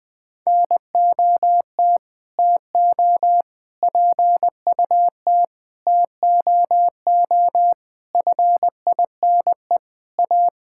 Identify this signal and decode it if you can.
Morse code, heard as a keyed tone, about 21 wpm, 700 Hz. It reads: NOT TO PUT TOO FINE A
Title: Sets of 9 Words - 20wpm